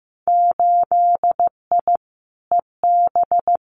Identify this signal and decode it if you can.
Morse code, heard as a keyed tone, about 15 wpm, 700 Hz. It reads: 8I EB
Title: Winter Field Day Exchanges Encore 15wpm